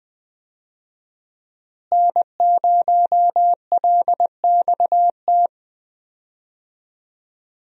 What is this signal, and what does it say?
Morse code, heard as a keyed tone, about 20 wpm, 700 Hz. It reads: N0LXT
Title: USA Call Signs - 5 Characters - Repeated 3x - 20wpm